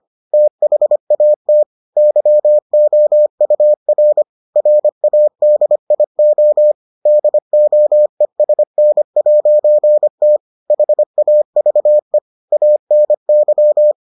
Morse code, heard as a keyed tone, about 25 wpm, 600 Hz. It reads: THAT YOUR RADIO DOESN'T HAVE ANY